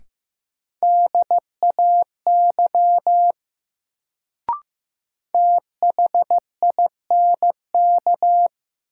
Morse code, heard as a keyed tone, about 15 wpm, 700 Hz. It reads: DAY THINK
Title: Sentences from Top 200 Words - New Word Drill - 15wpm